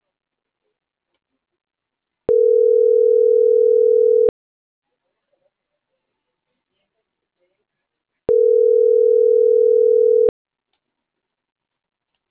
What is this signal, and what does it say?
Ringback tone (North America), 440 + 480 Hz.